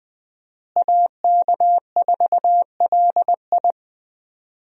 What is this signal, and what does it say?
Morse code, heard as a keyed tone, about 21 wpm, 700 Hz. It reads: AK4LI
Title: USA Call Signs - 5 Characters - Repeated 3x - 20wpm